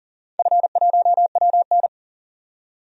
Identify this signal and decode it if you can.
Morse code, heard as a keyed tone, about 40 wpm, 700 Hz. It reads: F1WN
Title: DXCC-100 Call Signs - 4 Characters - 40wpm